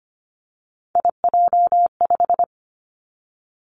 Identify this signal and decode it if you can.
Morse code, heard as a keyed tone, about 25 wpm, 700 Hz. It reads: IJ5
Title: Three Challenging Letter-Number Combinations 25wpm